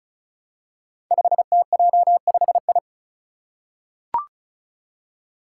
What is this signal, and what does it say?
Morse code, heard as a keyed tone, about 36 wpm, 700 Hz. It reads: 5TJ5I
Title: Five Challenging Letter-Number Combinations 35wpm